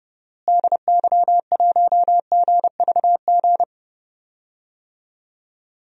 Morse code, heard as a keyed tone, about 31 wpm, 700 Hz. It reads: DY1GVG
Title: DXCC-100 Call Signs - 6 Characters - Repeated 2x - 30wpm